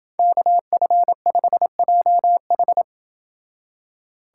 Morse code, heard as a keyed tone, about 29 wpm, 700 Hz. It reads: KF5JH